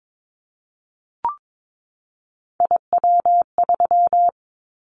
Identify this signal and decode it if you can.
Morse code, heard as a keyed tone, about 22 wpm, 700 Hz. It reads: IW3
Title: Callsign Prefixes for Top 25 DXCC - 22wpm